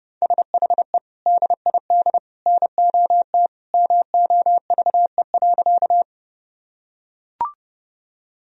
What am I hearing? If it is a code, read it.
Morse code, heard as a keyed tone, about 32 wpm, 700 Hz. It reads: SHE DID NOT MOVE.
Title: Sentences from Top 1000 Words - Review Part 1 - 30wpm